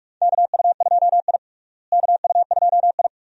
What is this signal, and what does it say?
Morse code, heard as a keyed tone, about 41 wpm, 700 Hz. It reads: KU2I KU2I
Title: USA Call Signs - 4 Characters - Repeated 2x - 45wpm